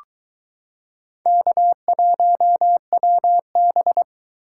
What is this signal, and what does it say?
Morse code, heard as a keyed tone, about 22 wpm, 700 Hz. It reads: K1WB